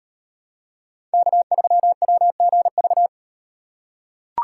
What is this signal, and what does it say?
Morse code, heard as a keyed tone, about 36 wpm, 700 Hz. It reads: K3WGV